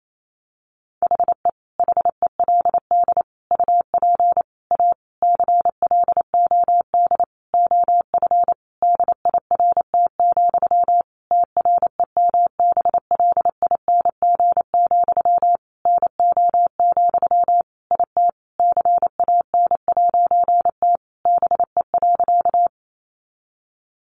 Morse code, heard as a keyed tone, about 28 wpm, 700 Hz. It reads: HE HELD UP A CLOD OF DIRT, TREMBLING, NO, IT CAN'T BE.
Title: A Sound Of Thunder 28wpm